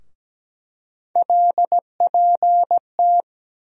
Morse code, heard as a keyed tone, about 17 wpm, 700 Hz. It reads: LPT
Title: USA Call Sign Postfixes - 3 Letters 17wpm